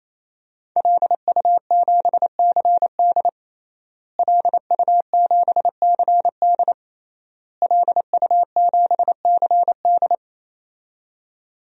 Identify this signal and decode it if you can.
Morse code, heard as a keyed tone, about 28 wpm, 700 Hz. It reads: LU7CD LU7CD LU7CD
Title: DXCC-100 Call Signs - 5 Characters - Repeated 3x - 28wpm